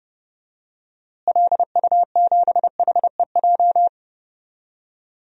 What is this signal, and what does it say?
Morse code, heard as a keyed tone, about 30 wpm, 700 Hz. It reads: LU7HEJ
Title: DXCC-100 Call Signs - 6 Characters - Repeated 3x - 30wpm